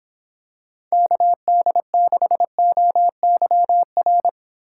Morse code, heard as a keyed tone, about 26 wpm, 700 Hz. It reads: KD6OYR